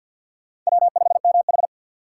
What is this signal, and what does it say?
Morse code, heard as a keyed tone, about 53 wpm, 700 Hz. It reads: W5MH
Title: USA Call Signs - 4 Characters - 6x Character Spacing - 50wpm